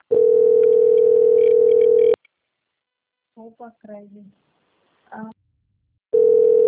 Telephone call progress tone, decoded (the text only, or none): ringback tone